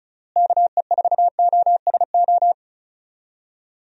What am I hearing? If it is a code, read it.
Morse code, heard as a keyed tone, about 36 wpm, 700 Hz. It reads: KE4OSO
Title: USA Call Signs - 6 Characters - Repeated 3x - 35wpm